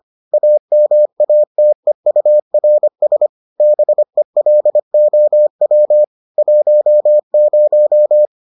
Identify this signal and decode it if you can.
Morse code, heard as a keyed tone, about 25 wpm, 600 Hz. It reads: AMATEURS BELOW 10